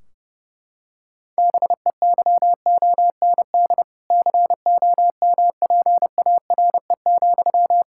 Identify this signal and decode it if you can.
Morse code, heard as a keyed tone, about 32 wpm, 700 Hz. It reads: BEYOND COMPARE,
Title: Sentences from Top 1600 Words 30wpm